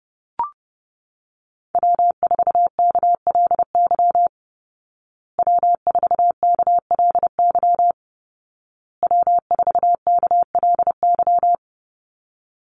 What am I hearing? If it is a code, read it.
Morse code, heard as a keyed tone, about 30 wpm, 700 Hz. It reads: W4KLY W4KLY W4KLY
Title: Call Signs Repeated 2x - 30wpm